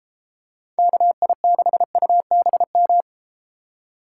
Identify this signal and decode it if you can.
Morse code, heard as a keyed tone, about 31 wpm, 700 Hz. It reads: KI6UBM